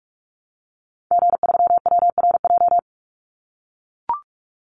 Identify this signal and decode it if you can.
Morse code, heard as a keyed tone, about 44 wpm, 700 Hz. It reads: G3WRJ
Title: Call Signs - 45wpm